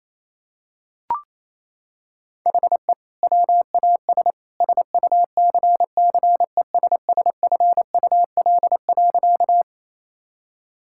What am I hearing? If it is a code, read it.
Morse code, heard as a keyed tone, about 28 wpm, 700 Hz. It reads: HE WAS SUCCESSFUL.
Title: Sentences from Top 1000 Words - Review Part 6 - 28wpm